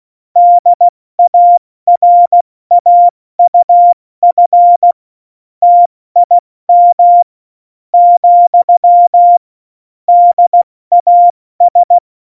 Morse code, heard as a keyed tone, about 16 wpm, spaced as for 14 wpm, 700 Hz. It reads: DARAUF TIM , DAS
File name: Morsekurs / Witze / Witz 14. Mai 2019, 70 F80